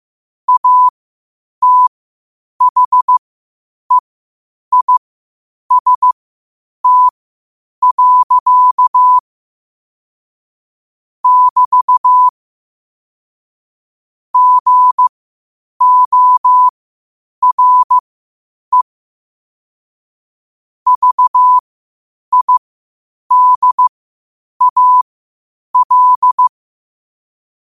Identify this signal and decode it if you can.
Morse code, heard as a keyed tone, about 15 wpm, spaced as for 8 wpm, 1000 Hz. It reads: ATHEIST. <BT> GORE VIDAL
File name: qotdfiles / 8wpm